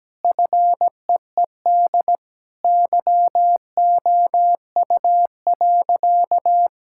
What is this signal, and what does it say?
Morse code, heard as a keyed tone, about 17 wpm, 700 Hz. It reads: FEED YOU.
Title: Sentences from Top 1600 Words 17wpm